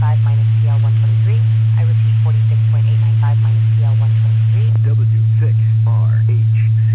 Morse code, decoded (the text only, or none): none